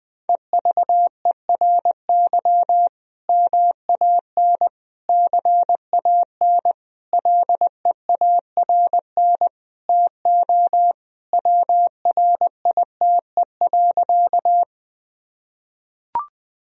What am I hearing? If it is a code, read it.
Morse code, heard as a keyed tone, about 21 wpm, 700 Hz. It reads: EVERY MAN CAN LEARN TO WRITE.
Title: Sentences from Top 1000 Words - Review Part 1 - 20wpm